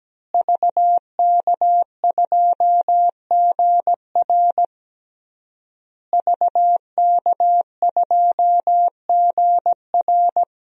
Morse code, heard as a keyed tone, about 17 wpm, 700 Hz. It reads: VK2GR VK2GR
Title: DXCC-100 Call Signs - 5 Characters - Repeated 3x - 17wpm